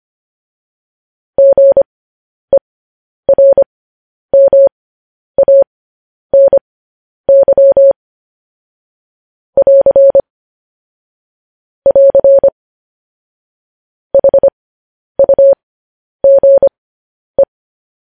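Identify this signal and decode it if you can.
Morse code, heard as a keyed tone, about 25 wpm, spaced as for 10 wpm, 550 Hz. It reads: GERMANY <AR> <AR> HUGE
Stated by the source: Morse news headlines sent at 10 words per minute.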